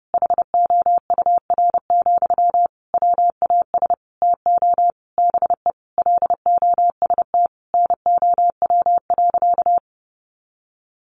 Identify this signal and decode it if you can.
Morse code, heard as a keyed tone, about 30 wpm, 700 Hz. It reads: HOUR, WAS TO BE LOST NOW.
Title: Sentences from Top 1400 Words 30wpm